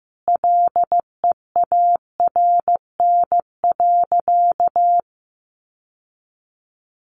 Morse code, heard as a keyed tone, about 15 wpm, 700 Hz. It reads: LEARN.
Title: Sentences from Top 800 Words 15wpm